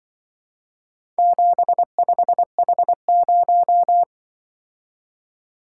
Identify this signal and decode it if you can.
Morse code, heard as a keyed tone, about 22 wpm, 700 Hz. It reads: 75H0
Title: Speed-Racing - Four Letter-Number Combinations - 20wpm